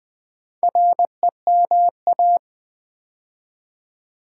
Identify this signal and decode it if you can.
Morse code, heard as a keyed tone, about 20 wpm, 700 Hz. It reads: REMA
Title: Build A Word - 2 Letters At A Time - 6-Letter Words - 20wpm